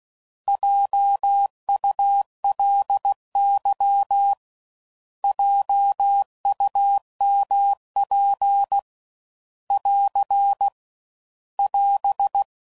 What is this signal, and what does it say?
Morse code, heard as a keyed tone, about 16 wpm, 800 Hz. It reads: JULY JUMP <AR> <AS>